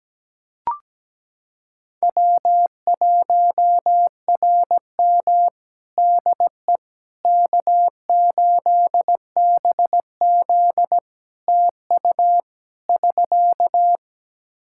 Morse code, heard as a keyed tone, about 17 wpm, 700 Hz. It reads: W1RM DE K8BZ TU <SK>
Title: Practice QSO - Exchange 4 - Ending - 17wpm